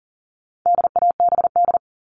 Morse code, heard as a keyed tone, about 40 wpm, 700 Hz. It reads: DABD
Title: All Cut Numbers - 4-Digit 40wpm